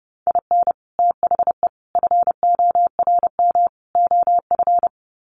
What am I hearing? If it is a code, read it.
Morse code, heard as a keyed tone, about 30 wpm, 700 Hz. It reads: IN THE FORM OF
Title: Sets of 4 Words 30wpm